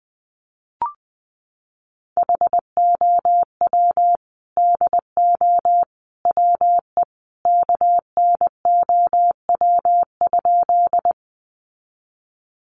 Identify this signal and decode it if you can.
Morse code, heard as a keyed tone, about 20 wpm, 700 Hz. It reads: HOW DO WE KNOW?
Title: A Sound Of Thunder 20wpm